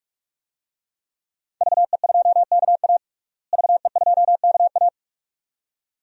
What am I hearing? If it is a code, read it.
Morse code, heard as a keyed tone, about 46 wpm, 700 Hz. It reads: VE2KA VE2KA
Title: DXCC-100 Call Signs - 5 Characters - Repeated 2x - 45wpm